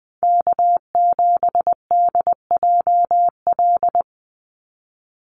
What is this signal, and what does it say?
Morse code, heard as a keyed tone, about 20 wpm, 700 Hz. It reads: K7DJL
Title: NA SOTA Chaser Callsigns 20wpm